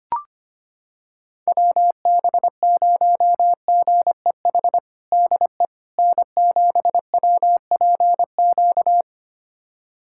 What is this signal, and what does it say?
Morse code, heard as a keyed tone, about 25 wpm, 700 Hz. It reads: WB0GEH DE N7WPQ
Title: USA Callsign DE Callsign 25wpm